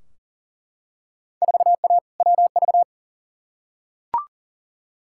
Morse code, heard as a keyed tone, about 37 wpm, 700 Hz. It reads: 4A WV
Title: Field Day Exchanges Encore - 40wpm